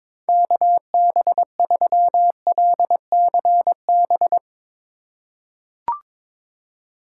Morse code, heard as a keyed tone, about 22 wpm, 700 Hz. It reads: KB3LCB
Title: USA Call Signs - 6 Characters - 2x Character Spacing - 22wpm